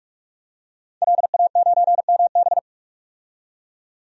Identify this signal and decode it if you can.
Morse code, heard as a keyed tone, about 43 wpm, 700 Hz. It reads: LA9MB
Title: DXCC-100 Call Signs - 5 Characters - 45wpm